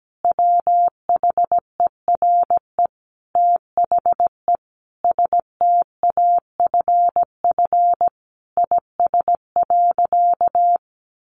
Morse code, heard as a keyed tone, about 17 wpm, 700 Hz. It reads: WHERE THE STAFF IS.
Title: Sentences from Top 1200 Words 17wpm